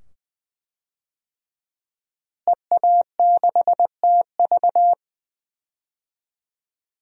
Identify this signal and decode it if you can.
Morse code, heard as a keyed tone, about 20 wpm, 700 Hz. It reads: EA6TV
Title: DXCC-100 Call Signs - 5 Characters - Repeated 2x - 20wpm